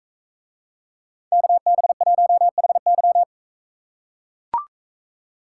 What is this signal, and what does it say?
Morse code, heard as a keyed tone, about 42 wpm, 700 Hz. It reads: KB1HY